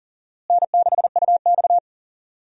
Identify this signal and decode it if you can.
Morse code, heard as a keyed tone, about 40 wpm, 700 Hz. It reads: N6UX
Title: USA Call Signs - 4 Characters - Repeated 2x - 40wpm